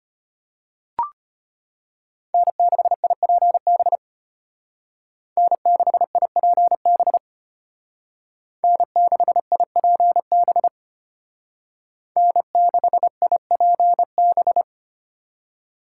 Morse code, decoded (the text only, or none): N6IPB N6IPB N6IPB N6IPB